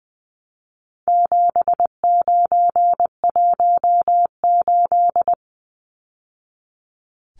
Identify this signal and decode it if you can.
Morse code, heard as a keyed tone, about 20 wpm, 700 Hz. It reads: 7918
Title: Numbers - 4-Digit 20wpm